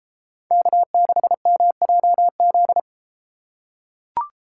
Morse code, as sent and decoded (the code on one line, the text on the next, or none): -.- -.... -- .--- --..
K6MJZ